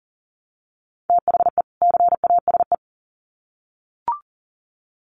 Morse code, heard as a keyed tone, about 40 wpm, 700 Hz. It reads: THE CASE
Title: Sets of 2 Words 40wpm